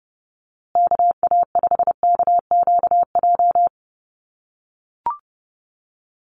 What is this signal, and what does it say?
Morse code, heard as a keyed tone, about 30 wpm, 700 Hz. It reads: KA5KQJ
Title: USA Call Signs Repeated 2x 30wpm